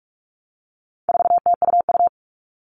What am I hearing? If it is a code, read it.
Morse code, heard as a keyed tone, about 44 wpm, 700 Hz. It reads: 4TUU
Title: All Cut Numbers - 4-Digit 45wpm